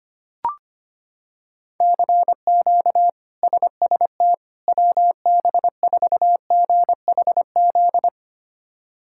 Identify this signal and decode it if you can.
Morse code, heard as a keyed tone, about 25 wpm, 700 Hz. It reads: CQ SST WB4GHZ
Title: K1USN SST QSO Practice - 25wpm